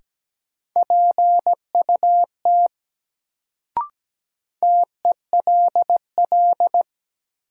Morse code, heard as a keyed tone, about 17 wpm, 700 Hz. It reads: PUT TELL
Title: Sentences from Top 200 Words - New Word Drill - 17wpm